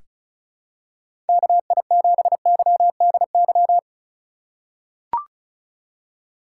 Morse code, heard as a keyed tone, about 34 wpm, 700 Hz. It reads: KI7YDY
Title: USA Call Signs - 3x Character Spacing - 35wpm